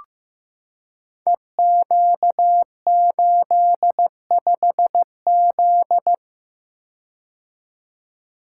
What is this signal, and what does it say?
Morse code, heard as a keyed tone, about 15 wpm, 700 Hz. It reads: EQ85Z
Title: Code Groups - 5 Letter:Numbers - 15wpm